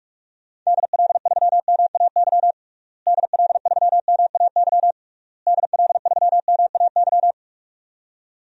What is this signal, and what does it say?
Morse code, heard as a keyed tone, about 43 wpm, 700 Hz. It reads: DL3MAY DL3MAY DL3MAY
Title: DXCC-100 Call Signs - 6 Characters - Repeated 3x - 45wpm